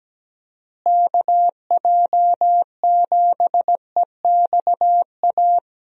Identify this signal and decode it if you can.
Morse code, heard as a keyed tone, about 17 wpm, 700 Hz. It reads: KJ7EXA
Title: USA Call Signs - 6 Characters - Repeated 2x - 17wpm